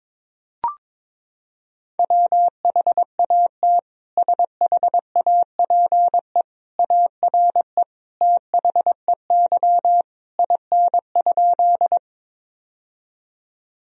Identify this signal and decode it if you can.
Morse code, heard as a keyed tone, about 21 wpm, 700 Hz. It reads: WHAT SHAPE ARE THEY IN?
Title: Sentences from Top 1000 Words - Review Part 3 - 22wpm